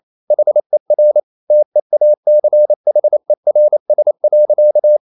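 Morse code, heard as a keyed tone, about 28 wpm, 600 Hz. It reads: HER TEACHERS.
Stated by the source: Morse code practice at 28 wpm, text of Ken Follett's novel Never in English, from CW med Gnister.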